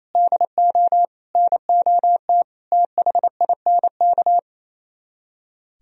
Morse code, heard as a keyed tone, about 33 wpm, 700 Hz. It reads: DO NOT THINK
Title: Speed-Racing - Sets of 5 Words 28wpm